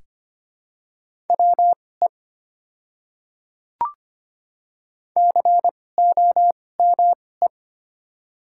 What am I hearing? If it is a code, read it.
Morse code, heard as a keyed tone, about 25 wpm, spaced as for 18 wpm, 700 Hz. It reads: WE COME
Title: 2x Character Spacing - Top 200 Words - 25wpm